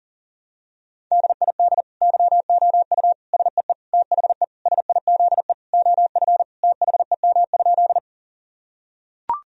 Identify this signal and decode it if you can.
Morse code, heard as a keyed tone, about 40 wpm, 700 Hz. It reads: DID YOU SEE THE SIZE OF THEM?
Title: Sentences from Top 1000 Words - Review Part 2 - 40wpm